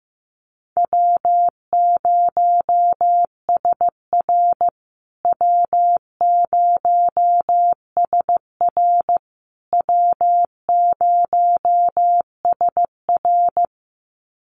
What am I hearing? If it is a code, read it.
Morse code, heard as a keyed tone, about 14 wpm, 700 Hz. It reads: W0SR W0SR W0SR
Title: USA Call Signs Repeated 3x 15wpm